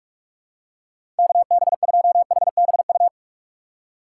Morse code, heard as a keyed tone, about 44 wpm, 700 Hz. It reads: KB2HBU
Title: USA Call Signs - 6 Characters - 3x Character Spacing - 45wpm